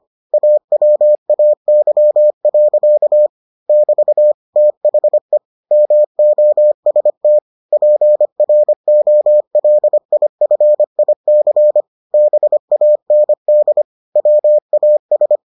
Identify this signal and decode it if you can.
Morse code, heard as a keyed tone, about 25 wpm, 600 Hz. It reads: AWAY. <BT> THE MOST PROLIFIC BAND WAS